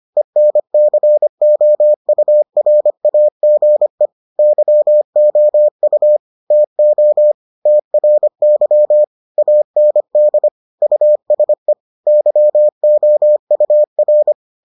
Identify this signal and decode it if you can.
Morse code, heard as a keyed tone, about 25 wpm, 600 Hz. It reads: ENCOURAGE YOU TO TRY AND USE YOUR